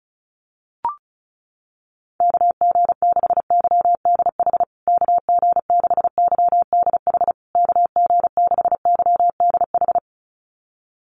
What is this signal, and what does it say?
Morse code, heard as a keyed tone, about 34 wpm, 700 Hz. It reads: KG6YDH KG6YDH KG6YDH
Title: USA Call Signs Repeated 3x 35wpm